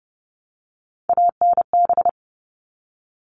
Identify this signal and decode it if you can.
Morse code, heard as a keyed tone, about 30 wpm, 700 Hz. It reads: ANB
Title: All Cut Numbers - 3-Digit 30wpm